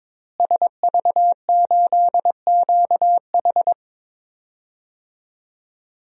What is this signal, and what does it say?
Morse code, heard as a keyed tone, about 22 wpm, 700 Hz. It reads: SV8QH
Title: DXCC-100 Call Signs - 5 Characters - Repeated 2x - 22wpm